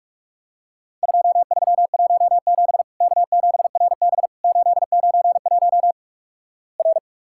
Morse code, heard as a keyed tone, about 47 wpm, 700 Hz. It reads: 2317 K7RB 891 R
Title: Worked All Europe DX Contest - QTC Exchange - 45wpm